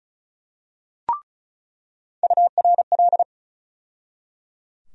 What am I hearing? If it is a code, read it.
Morse code, heard as a keyed tone, about 36 wpm, 700 Hz. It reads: URL
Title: USA Call Sign Postfixes - 3 Letters 35wpm